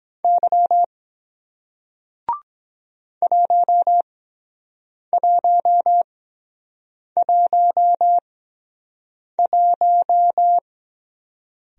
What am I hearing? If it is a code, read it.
Morse code, heard as a keyed tone, about 23 wpm, spaced as for 7 wpm, 700 Hz. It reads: Y 1111